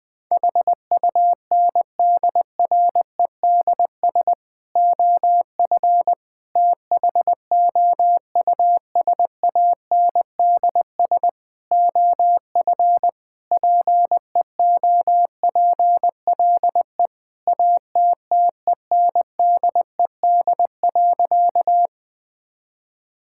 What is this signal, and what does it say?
Morse code, heard as a keyed tone, about 20 wpm, 700 Hz. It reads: HUNDREDS OF THOUSANDS OF PEOPLE ATTENDED.
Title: Sentences with 4-Word Phrases - 20wpm